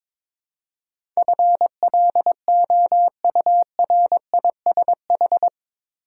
Morse code, heard as a keyed tone, about 22 wpm, 700 Hz. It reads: FLOURISH
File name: practice / 22wpm / Eight Letter Words - 22wpm